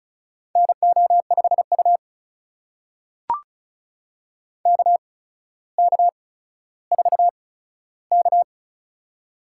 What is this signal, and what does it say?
Morse code, heard as a keyed tone, about 36 wpm, spaced as for 15 wpm, 700 Hz. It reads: NO5U KK4K
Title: USA Call Signs - 4 Characters - 8x Character Spacing - 35wpm